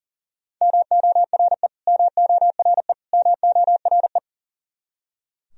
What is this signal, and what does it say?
Morse code, heard as a keyed tone, about 40 wpm, 700 Hz. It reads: MORE MORE MORE
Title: Top 100 Words Repeated 40wpm